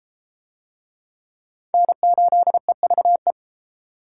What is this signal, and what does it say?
Morse code, heard as a keyed tone, about 33 wpm, 700 Hz. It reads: N8EVE